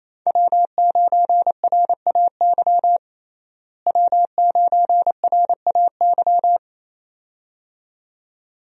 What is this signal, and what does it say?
Morse code, heard as a keyed tone, about 28 wpm, 700 Hz. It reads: W9RAY W9RAY
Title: USA Call Signs - 5 Characters - Repeated 2x - 28wpm